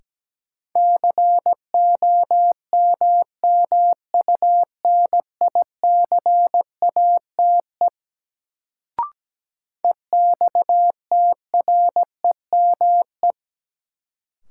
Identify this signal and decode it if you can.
Morse code, heard as a keyed tone, about 17 wpm, 700 Hz. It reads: COMMUNICATE EXTREME
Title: Sentences from Top 1300 Words - New Word Drill - 17wpm